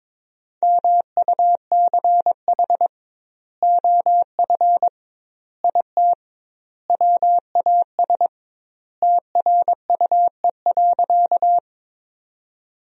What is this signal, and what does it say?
Morse code, heard as a keyed tone, about 23 wpm, 700 Hz. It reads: MUCH OF IT WAS TRUE.
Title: 2x Word-Spacing - Sentences from Top 400 Words - 22wpm